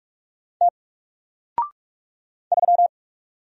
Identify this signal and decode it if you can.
Morse code, heard as a keyed tone, about 45 wpm, 700 Hz. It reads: T 3
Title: Single Character (Letter:Number:Punctuation:Prosign) Alt 45wpm